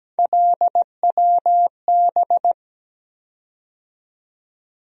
Morse code, heard as a keyed tone, about 17 wpm, 700 Hz. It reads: LWB
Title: USA Call Sign Postfixes - 3 Letters 17wpm